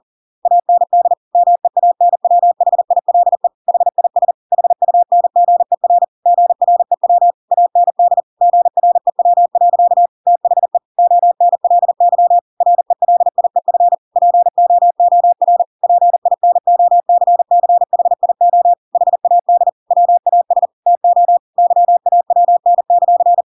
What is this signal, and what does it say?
Morse code, heard as a keyed tone, about 40 wpm, 700 Hz. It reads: AND MEANWHILE HIS HUNGER GREW AND GREW. THE ONLY RELIEF POOR PINOCCHIO HAD WAS TO YAWN;
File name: Pinocchio / Pinocchio ch5 40 40 ik1hge